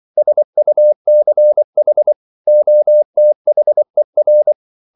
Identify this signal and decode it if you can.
Morse code, heard as a keyed tone, about 24 wpm, 600 Hz. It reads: SUCH OTHER
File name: CW / Practice / 100MostCommonEnglishWords24